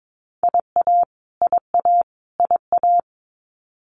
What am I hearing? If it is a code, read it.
Morse code, heard as a keyed tone, about 22 wpm, 700 Hz. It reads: IA IA IA
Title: US States Repeated 3x 22wpm